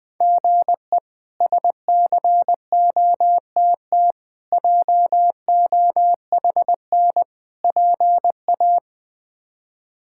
Morse code, heard as a keyed tone, about 20 wpm, 700 Hz. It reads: GE SCOTT JOHN PA
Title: K1USN SST QSO Practice - 20wpm